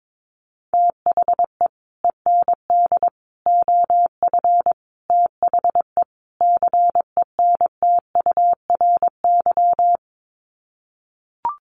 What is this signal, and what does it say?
Morse code, heard as a keyed tone, about 22 wpm, 700 Hz. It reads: THE END OF THE CENTURY
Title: Sets of 5 Words Encore2 22wpm